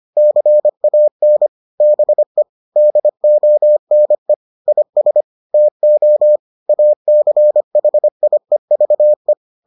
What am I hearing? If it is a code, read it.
Morse code, heard as a keyed tone, about 25 wpm, 600 Hz. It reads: CAN BE DONE IS TO ACHIEVE